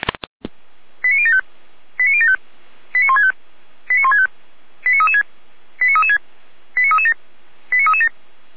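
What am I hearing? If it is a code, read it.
Five-tone selective call (ZVEI), heard as quick runs of five tones, tones of about 80 ms. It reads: 89075 89075 89165 89165 89336 89336 89337 89337
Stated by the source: Five-tone ZVEI selective call 89223 received on an SDR, from the feuerwehr-orlamuende.org archive.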